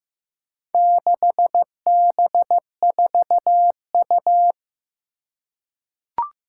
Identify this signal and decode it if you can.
Morse code, heard as a keyed tone, about 15 wpm, 700 Hz. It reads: 6B4U
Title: Four Challenging Letter-Number Combinations 15wpm